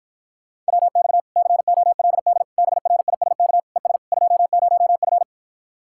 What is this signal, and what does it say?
Morse code, heard as a keyed tone, about 69 wpm, 700 Hz. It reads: WX COLD BRISK ES 20F